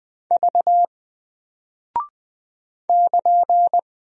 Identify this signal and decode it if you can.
Morse code, heard as a keyed tone, about 20 wpm, 700 Hz. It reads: V <KN>
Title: Single Character (Letter:Number:Punctuation:Prosign) 20wpm